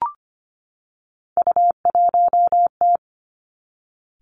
Morse code, heard as a keyed tone, about 25 wpm, 700 Hz. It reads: U1T
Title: Three Challenging Letter-Number Combinations 25wpm